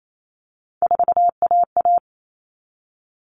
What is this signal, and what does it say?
Morse code, heard as a keyed tone, about 28 wpm, 700 Hz. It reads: 4AA
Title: All Cut Numbers - 3-Digit 28wpm